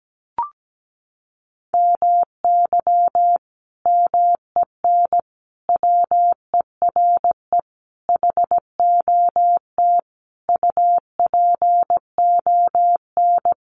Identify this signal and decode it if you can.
Morse code, heard as a keyed tone, about 17 wpm, 700 Hz. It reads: MY MEN WERE HOT UPON
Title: Sentences from Top 1100 Words 17wpm